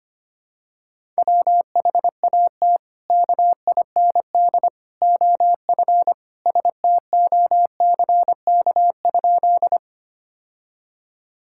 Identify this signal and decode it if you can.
Morse code, heard as a keyed tone, about 25 wpm, 700 Hz. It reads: WHAT KIND OF STOCK?
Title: Sentences from Top 1000 Words - Review Part 5 - 25wpm